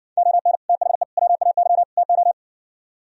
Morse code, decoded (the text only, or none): ON THE WAY TO